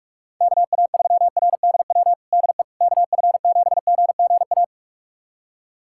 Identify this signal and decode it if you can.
Morse code, heard as a keyed tone, about 46 wpm, 700 Hz. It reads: KA3RDW DE KF7GGA
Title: USA Callsign DE Callsign 45wpm